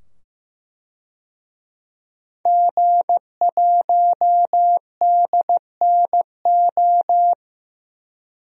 Morse code, heard as a keyed tone, about 15 wpm, 700 Hz. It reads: G1DNO
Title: DXCC-100 Call Signs - 5 Characters - 15wpm